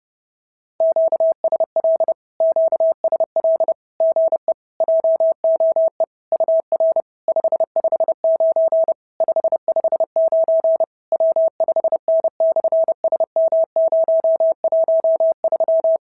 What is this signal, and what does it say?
Morse code, heard as a keyed tone, about 30 wpm, 690 Hz. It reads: QSL QSL GE JOE UR 559 559 W5N/SM013
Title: SOTA QSO Practice 30wpm